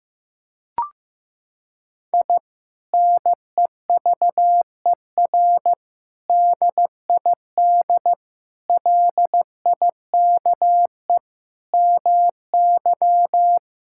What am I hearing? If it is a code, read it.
Morse code, heard as a keyed tone, about 15 wpm, 700 Hz. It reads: I NEVER DID LIKE MY
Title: Sentences from Top 1000 Words - Review Part 5 - 15wpm